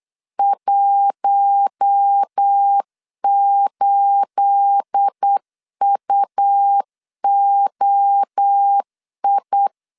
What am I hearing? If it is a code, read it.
Morse code, heard as a keyed tone, about 8 wpm, 800 Hz. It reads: 18UOI
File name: LEZIONI / 10 SIGNS / 10A SLASH 8wpm 3452